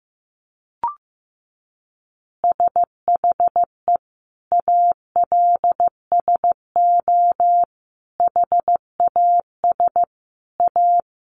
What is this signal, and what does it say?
Morse code, heard as a keyed tone, about 15 wpm, 700 Hz. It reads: SHE ALSO HAS A
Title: Easy Phrases 15wpm